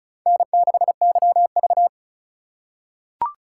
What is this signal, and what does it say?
Morse code, heard as a keyed tone, about 36 wpm, 700 Hz. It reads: N6YV
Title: USA Call Signs - 4 Characters - 4x Character Spacing - 35wpm